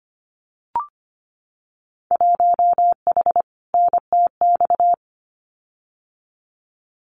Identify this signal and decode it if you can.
Morse code, heard as a keyed tone, about 25 wpm, 700 Hz. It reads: 1H NTX
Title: Winter Field Day Exchanges 25wpm